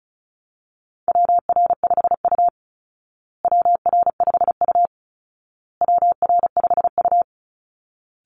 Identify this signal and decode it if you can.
Morse code, heard as a keyed tone, about 34 wpm, 700 Hz. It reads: WR5U WR5U WR5U
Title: Call Signs Repeated 2x - 35wpm